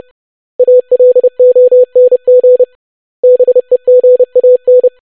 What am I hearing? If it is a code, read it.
Morse code, heard as a keyed tone, about 30 wpm, 500 Hz. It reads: ALONG BEGAN
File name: basic500-25wpm